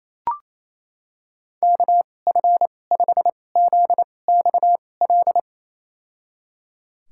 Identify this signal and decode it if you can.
Morse code, heard as a keyed tone, about 28 wpm, spaced as for 20 wpm, 700 Hz. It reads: KF5ZXL
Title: USA Call Signs - 6 Characters - 2x Character Spacing - 28wpm